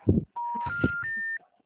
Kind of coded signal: telephone call progress tone

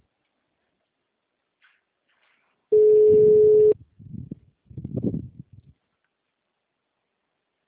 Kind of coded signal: telephone call progress tone

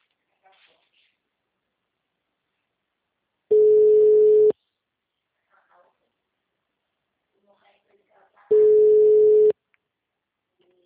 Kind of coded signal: telephone call progress tone

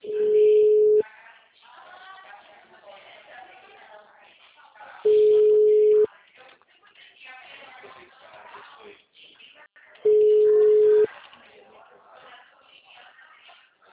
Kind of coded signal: telephone call progress tone

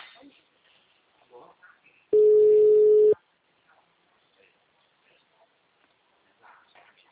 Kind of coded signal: telephone call progress tone

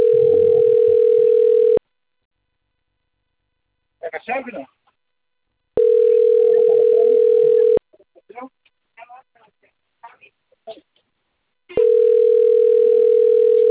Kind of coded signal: telephone call progress tone

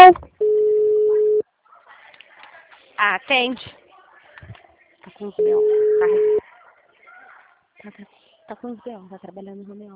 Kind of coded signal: telephone call progress tone

